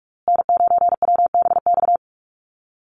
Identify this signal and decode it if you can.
Morse code, heard as a keyed tone, about 44 wpm, 700 Hz. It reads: N9WBX
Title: USA Call Signs Repeated 2x 45wpm